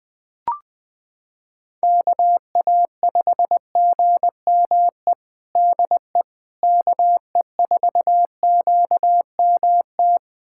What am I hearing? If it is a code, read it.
Morse code, heard as a keyed tone, about 20 wpm, 700 Hz. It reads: KA5GME DE KE4QMT
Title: USA Callsign DE Callsign - 6 Characters - 20wpm